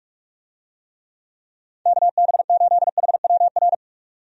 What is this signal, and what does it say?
Morse code, heard as a keyed tone, about 46 wpm, 700 Hz. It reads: KB8HWR
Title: USA Call Signs - 6 Characters - 45wpm